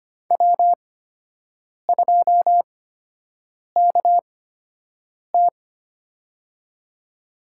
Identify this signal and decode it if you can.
Morse code, heard as a keyed tone, about 25 wpm, spaced as for 7 wpm, 700 Hz. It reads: W2KT
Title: USA Call Signs - 4 Characters - 8x Character Spacing - 25wpm